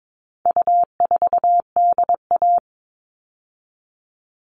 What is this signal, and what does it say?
Morse code, heard as a keyed tone, about 22 wpm, 700 Hz. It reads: U4DA
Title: All Cut Numbers - 4-Digit 22wpm